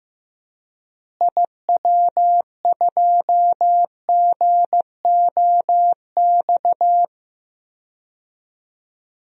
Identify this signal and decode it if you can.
Morse code, heard as a keyed tone, about 15 wpm, 700 Hz. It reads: IW2GOX
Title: DXCC-100 Call Signs - 6 Characters - Repeated 2x - 15wpm